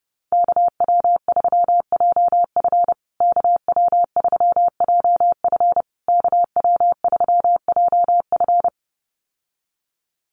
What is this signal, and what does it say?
Morse code, heard as a keyed tone, about 30 wpm, 700 Hz. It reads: KW3JF KW3JF KW3JF
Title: USA Call Signs Repeated 3x 30wpm